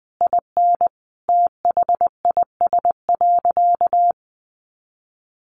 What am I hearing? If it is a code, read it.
Morse code, heard as a keyed tone, about 19 wpm, 700 Hz. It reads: IN THIS.
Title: Sentences from Top 1300 Words 20wpm